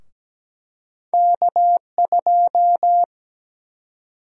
Morse code, heard as a keyed tone, about 17 wpm, 700 Hz. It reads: K2
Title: USA Call Sign Prefixes - 17wpm